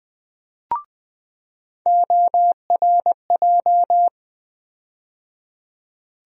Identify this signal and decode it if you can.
Morse code, heard as a keyed tone, about 20 wpm, 700 Hz. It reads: ORJ
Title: USA Call Sign Postfixes - 3 Letters 20wpm